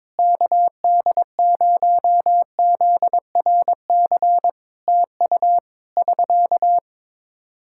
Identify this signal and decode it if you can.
Morse code, heard as a keyed tone, about 22 wpm, 700 Hz. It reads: KD0ZRC TU <SK>
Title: K1USN SST QSO Practice 22wpm